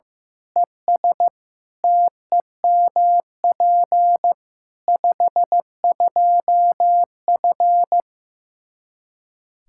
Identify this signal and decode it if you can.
Morse code, heard as a keyed tone, about 15 wpm, 700 Hz. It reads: ES TEMP 52F
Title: Practice QSO - Exchange 2 - Info - 15wpm